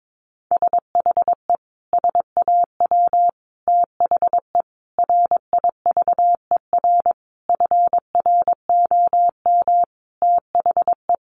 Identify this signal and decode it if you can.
Morse code, heard as a keyed tone, about 22 wpm, 700 Hz. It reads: SHE SAW THE RIVER FROM THE
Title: Sentences from Top 1200 Words 22wpm